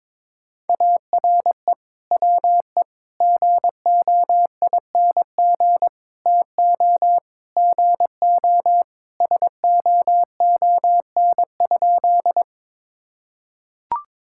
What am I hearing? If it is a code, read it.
Morse code, heard as a keyed tone, about 22 wpm, 700 Hz. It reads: ARE WE GOING TO GO SOON?
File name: practice / 22wpm / Sentences with 3-Word Phrases - 22wpm